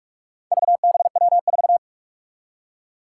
Morse code, heard as a keyed tone, about 42 wpm, 700 Hz. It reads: VBW4
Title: Four Challenging Letter-Number Combinations 45wpm